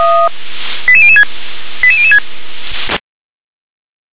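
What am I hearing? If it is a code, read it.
Five-tone selective call (ZVEI), heard as quick runs of five tones, tones of about 70 ms. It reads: 88006 88006
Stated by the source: ZVEI five-tone call 88177 received by SDR.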